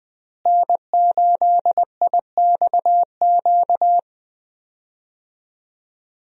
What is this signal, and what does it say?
Morse code, heard as a keyed tone, about 20 wpm, 700 Hz. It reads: N8IXQ
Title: USA Call Signs - 5 Characters - Repeated 2x - 20wpm